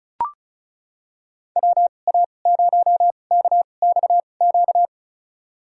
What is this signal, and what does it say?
Morse code, heard as a keyed tone, about 36 wpm, spaced as for 25 wpm, 700 Hz. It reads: WA0KXQ
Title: USA Call Signs - 6 Characters - 2x Character Spacing - 35wpm